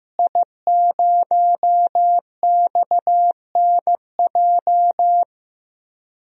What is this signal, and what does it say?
Morse code, heard as a keyed tone, about 15 wpm, 700 Hz. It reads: I0XNJ
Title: DXCC-100 Call Signs - 5 Characters - Repeated 2x - 15wpm